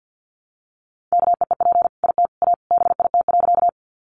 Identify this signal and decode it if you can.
Morse code, heard as a keyed tone, about 50 wpm, 700 Hz. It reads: KEEP IT A BIT.
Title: Sentences from Top 900 Words 50wpm